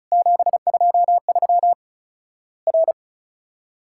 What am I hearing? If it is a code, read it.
Morse code, heard as a keyed tone, about 36 wpm, 700 Hz. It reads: 723 R
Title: Worked All Europe DX Contest - QTC Exchange - 35wpm